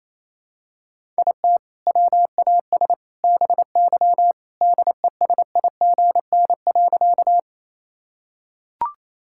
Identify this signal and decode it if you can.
Morse code, heard as a keyed tone, about 27 wpm, 700 Hz. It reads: IT WAS BY DESIGN.
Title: Sentences from Top 1000 Words - Review Part 5 - 28wpm